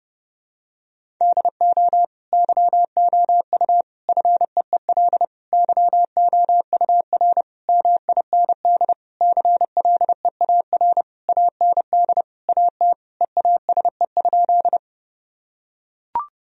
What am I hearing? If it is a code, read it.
Morse code, heard as a keyed tone, about 30 wpm, 700 Hz. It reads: DO YOU FEEL YOUR MIND CLEAR AND AT EASE?
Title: Sentences from Top 1000 Words - Review Part 1 - 30wpm